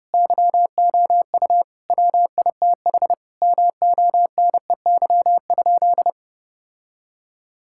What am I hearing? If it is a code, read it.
Morse code, heard as a keyed tone, about 30 wpm, 700 Hz. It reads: YOU WITH MONEY?
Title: Sentences from Top 1600 Words 30wpm